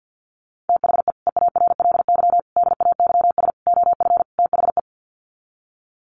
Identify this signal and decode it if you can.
Morse code, heard as a keyed tone, about 49 wpm, 700 Hz. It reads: THE EARLY DAYS OF THE
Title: Sets of 5 Words Encore 50wpm